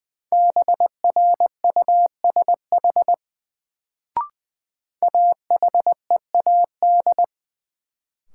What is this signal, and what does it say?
Morse code, heard as a keyed tone, about 20 wpm, 700 Hz. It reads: BRUSH AHEAD
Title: Sentences from Top 1100 Words - New Word Drill Alt - 20wpm